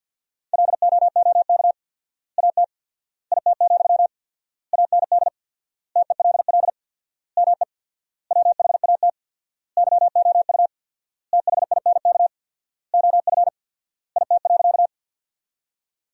Morse code, heard as a keyed tone, about 52 wpm, 700 Hz. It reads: LOOK AT IT, AND TELL ME WHAT YOU THINK OF IT.
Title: 4x Word-Spacing - Sentences from Top 200 Words - 50wpm